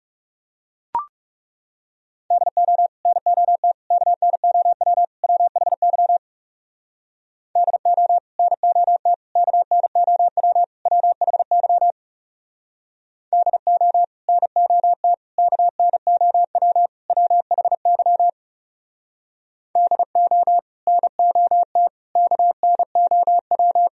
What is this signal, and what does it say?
Morse code, heard as a keyed tone, about 39 wpm, 700 Hz. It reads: DO NOT KNOW WHY DO NOT KNOW WHY DO NOT KNOW WHY DO NOT KNOW